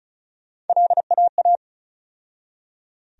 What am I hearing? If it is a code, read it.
Morse code, heard as a keyed tone, about 34 wpm, 700 Hz. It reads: LAA
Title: USA Call Sign Postfixes - 3 Letters 35wpm